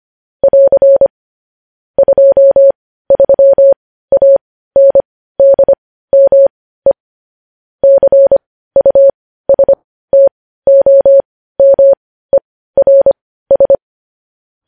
Morse code, heard as a keyed tone, about 25 wpm, spaced as for 15 wpm, 550 Hz. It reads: <AR> 23ANDME CUSTOMERS
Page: News Headlines in Morse Code at 15 WPM